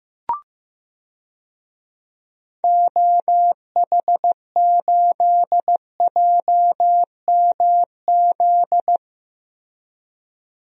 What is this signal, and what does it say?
Morse code, heard as a keyed tone, about 15 wpm, 700 Hz. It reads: OH8JMZ
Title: DXCC-100 Call Signs - 6 Characters - Repeated 2x - 15wpm